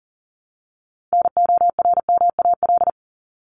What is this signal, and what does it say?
Morse code, heard as a keyed tone, about 40 wpm, 700 Hz. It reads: NORMAL
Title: LY Suffix - Top 100 Words with Base Words 40wpm